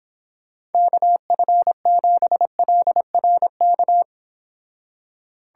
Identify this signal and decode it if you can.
Morse code, heard as a keyed tone, about 29 wpm, 700 Hz. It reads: KF7LRK